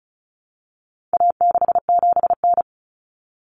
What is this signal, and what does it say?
Morse code, heard as a keyed tone, about 34 wpm, 700 Hz. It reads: A67N
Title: Common Cut Numbers - 4-Digit 35wpm